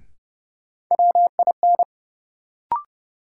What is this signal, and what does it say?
Morse code, heard as a keyed tone, about 32 wpm, 700 Hz. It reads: WIN
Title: Sentences from Top 1500 Words - New Word Drill - 30wpm